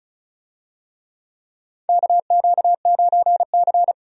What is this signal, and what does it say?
Morse code, heard as a keyed tone, about 36 wpm, 700 Hz. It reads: KQ9C
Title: USA Call Signs - 4 Characters - 35wpm